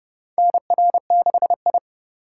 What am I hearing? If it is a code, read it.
Morse code, heard as a keyed tone, about 32 wpm, 700 Hz. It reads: NR6I
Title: USA Call Signs - 4 Characters - 2x Character Spacing - 30wpm